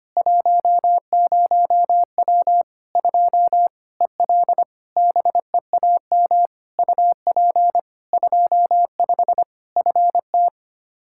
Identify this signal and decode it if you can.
Morse code, heard as a keyed tone, about 25 wpm, 700 Hz. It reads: 10W 2 EL BEAM UP 25 FT
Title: Practice QSO - Exchange 2 - Info - 25wpm